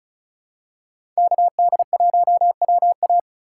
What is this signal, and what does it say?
Morse code, heard as a keyed tone, about 34 wpm, 700 Hz. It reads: KD1WA
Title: USA Call Signs - 5 Characters - 3x Character Spacing - 35wpm